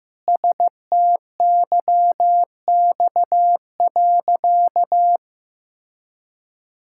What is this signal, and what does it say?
Morse code, heard as a keyed tone, about 15 wpm, 700 Hz. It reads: STYX.
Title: Pangram Challenge v2 - 15wpm